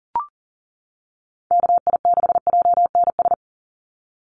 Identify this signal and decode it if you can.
Morse code, heard as a keyed tone, about 40 wpm, 700 Hz. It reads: KI6JNS
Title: USA Call Signs 40wpm